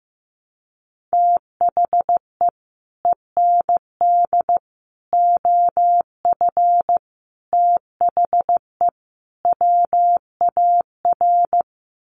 Morse code, heard as a keyed tone, about 15 wpm, 700 Hz. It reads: THE END OF THE WAR
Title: Sets of 5 Words 15wpm